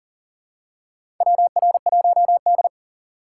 Morse code, heard as a keyed tone, about 40 wpm, 700 Hz. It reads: WR1D
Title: USA Call Signs - 4 Characters - 6x Character Spacing - 40wpm